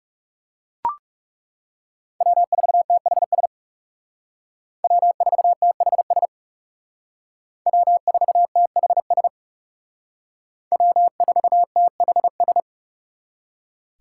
Morse code, heard as a keyed tone, about 36 wpm, 700 Hz. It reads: W4THS W4THS W4THS W4THS
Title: Speed Racing - USA 1x3 Call Signs 30wpm